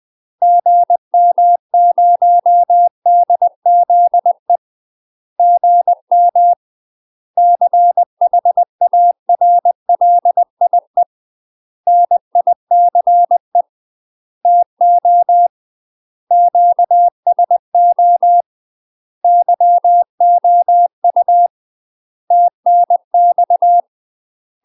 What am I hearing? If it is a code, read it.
Morse code, heard as a keyed tone, about 20 wpm, 700 Hz. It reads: GM0DZE GM CHARLIE NICE TO QSO YOU TNX
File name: UK MASTER QSOs TRAINING/20WPM